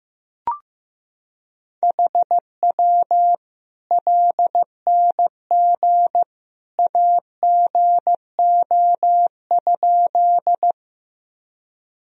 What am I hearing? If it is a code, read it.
Morse code, heard as a keyed tone, about 15 wpm, 700 Hz. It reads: HW LNG AGO?
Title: Sentences from Top 500 Words and Abbreviations 15wpm